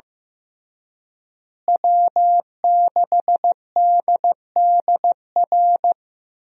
Morse code, heard as a keyed tone, about 16 wpm, 700 Hz. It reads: W6DDR
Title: USA Call Signs - 5 Characters - Repeated 3x - 15wpm